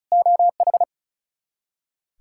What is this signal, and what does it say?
Morse code, heard as a keyed tone, about 36 wpm, 700 Hz. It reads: OH
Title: K1USN SST QSO Practice 35wpm